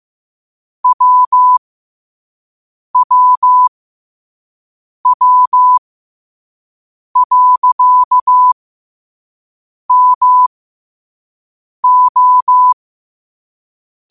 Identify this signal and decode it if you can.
Morse code, heard as a keyed tone, about 15 wpm, spaced as for 5 wpm, 1000 Hz. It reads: WWW.MO
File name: qotdfiles / 5wpm